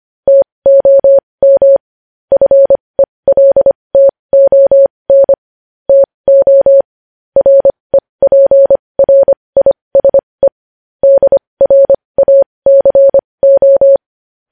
Morse code, heard as a keyed tone, about 25 wpm, spaced as for 20 wpm, 550 Hz. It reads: TOM FELTON TO REPRISE DRACO
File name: podcast / News headlines-1x-25x20